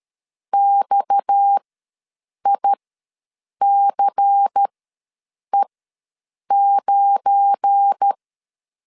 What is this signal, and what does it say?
Morse code, heard as a keyed tone, about 13 wpm, spaced as for 7 wpm, 800 Hz. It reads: XICE9
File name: LEZIONI / 7 GF9Z / 7 GF9 ws 12wpm 3344